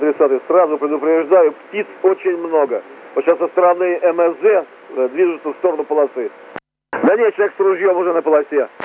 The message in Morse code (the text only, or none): none